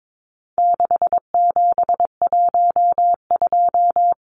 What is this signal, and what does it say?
Morse code, heard as a keyed tone, about 22 wpm, 700 Hz. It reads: 6712
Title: Numbers - 4-Digit 22wpm